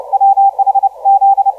Morse code, heard as a keyed tone, about 30 wpm, 780 Hz, with radio noise. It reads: WH Z
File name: 5U9AMO 1512UTC 12m